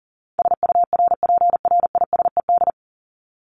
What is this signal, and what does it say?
Morse code, heard as a keyed tone, about 40 wpm, 700 Hz. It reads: SURPRISED
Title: Top 1501-1750 Words 40wpm